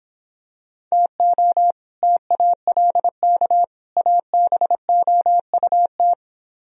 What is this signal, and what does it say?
Morse code, heard as a keyed tone, about 26 wpm, 700 Hz. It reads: TO TALK ABOUT